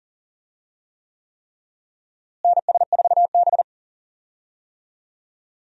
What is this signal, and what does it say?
Morse code, heard as a keyed tone, about 40 wpm, 700 Hz. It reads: NS4B
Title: USA Call Signs - 4 Characters - Repeated 3x - 40wpm